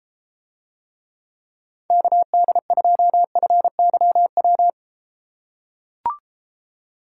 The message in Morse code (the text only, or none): KD2FYW